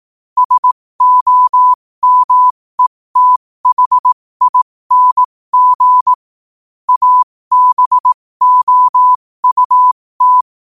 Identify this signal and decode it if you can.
Morse code, heard as a keyed tone, about 18 wpm, spaced as for 15 wpm, 1000 Hz. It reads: SOMETHING ABOUT
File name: qotdfiles / 15wpm